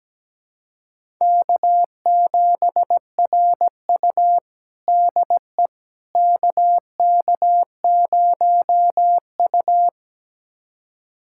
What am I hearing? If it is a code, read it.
Morse code, heard as a keyed tone, about 17 wpm, 700 Hz. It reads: K7RU DE KK0U
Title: USA Callsign DE Callsign - 4 Characters - 17wpm